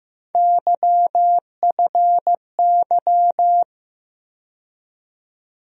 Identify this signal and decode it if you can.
Morse code, heard as a keyed tone, about 15 wpm, 700 Hz. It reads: YFY
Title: USA Call Sign Postfixes - 3 Letters 15wpm